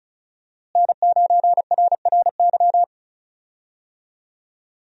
Morse code, heard as a keyed tone, about 36 wpm, 700 Hz. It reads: N9RRY
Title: USA Call Signs - 5 Characters - Repeated 2x - 35wpm